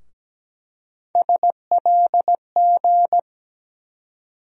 Morse code, heard as a keyed tone, about 17 wpm, 700 Hz. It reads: SLG
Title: USA Call Sign Postfixes - 3 Letters 17wpm